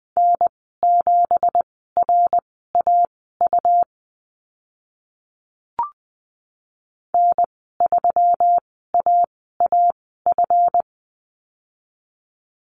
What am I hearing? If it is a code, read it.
Morse code, heard as a keyed tone, about 20 wpm, spaced as for 14 wpm, 700 Hz. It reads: N7RAU N3AAF
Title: 193 Lesson 31g - Letter v - Call Signs 20wpm